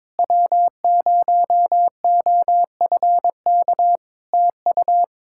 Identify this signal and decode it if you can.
Morse code, heard as a keyed tone, about 22 wpm, 700 Hz. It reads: W0OFK TU
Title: K1USN SST QSO Practice 22wpm